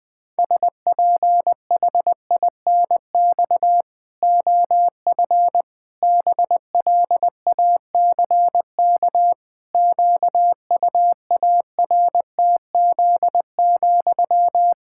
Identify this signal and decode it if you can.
Morse code, heard as a keyed tone, about 20 wpm, 700 Hz. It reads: SPHINX OF BLACK QUARTZ,
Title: Pangram Challenge v2 - 20wpm